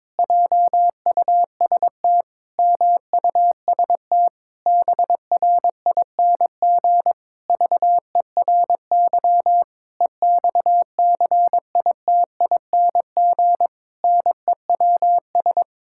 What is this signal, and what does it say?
Morse code, heard as a keyed tone, about 22 wpm, 700 Hz. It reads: JUST MUST BRING VERY EXCITING NEWS
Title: Pangram Challenge v2 - 22wpm